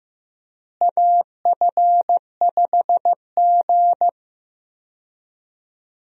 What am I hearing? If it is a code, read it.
Morse code, heard as a keyed tone, about 15 wpm, 700 Hz. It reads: AF5G
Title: USA Call Signs - 4 Characters - Repeated 2x - 15wpm